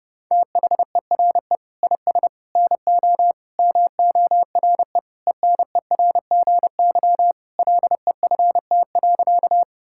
Morse code, heard as a keyed tone, about 30 wpm, 700 Hz. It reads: THERE IS NO MORE ENERGY LEFT.
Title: Sentences from Top 1000 Words - Review Part 6 - 30wpm